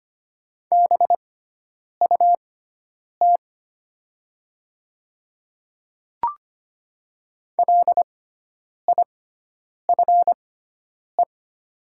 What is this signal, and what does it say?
Morse code, heard as a keyed tone, about 25 wpm, spaced as for 9 wpm, 700 Hz. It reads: BUT LIFE
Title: 6x Character Spacing - Top 300 Words - 25wpm